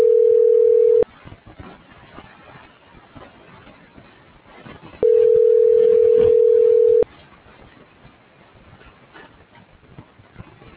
Ringback tone (North America), 440 + 480 Hz.